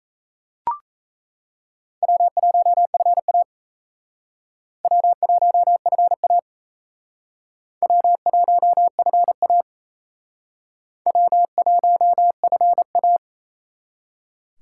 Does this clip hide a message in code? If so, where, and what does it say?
Yes: Morse code from 2.0 to 13.2 s, heard as a keyed tone, about 37 wpm, 700 Hz. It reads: W1FA W1FA W1FA W1FA